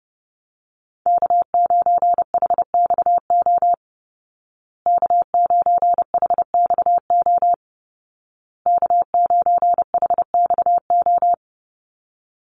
Morse code, heard as a keyed tone, about 30 wpm, 700 Hz. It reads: K9HXO K9HXO K9HXO
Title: Call Signs Repeated 2x - 30wpm